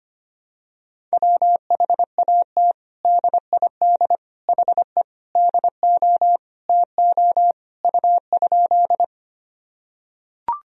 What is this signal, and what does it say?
Morse code, heard as a keyed tone, about 25 wpm, 700 Hz. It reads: WHAT DID HE DO TO U?
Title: Sentences from Top 100 Words and Abbreviations 25wpm